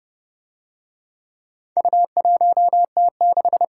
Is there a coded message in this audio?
Yes: Morse code from 1.8 to 3.6 s, heard as a keyed tone, about 27 wpm, 700 Hz. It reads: U1T6